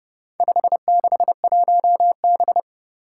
Morse code, heard as a keyed tone, about 30 wpm, 700 Hz. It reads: 561B
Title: Four Challenging Letter-Number Combinations 30wpm